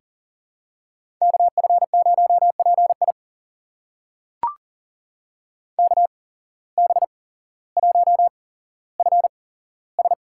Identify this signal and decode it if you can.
Morse code, heard as a keyed tone, about 38 wpm, spaced as for 17 wpm, 700 Hz. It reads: KF0PI KB1FS
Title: USA Call Signs - 8x Character Spacing - 40wpm